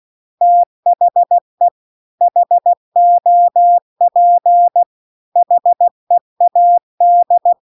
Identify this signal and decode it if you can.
Morse code, heard as a keyed tone, about 16 wpm, 700 Hz. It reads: THE HOP HEAD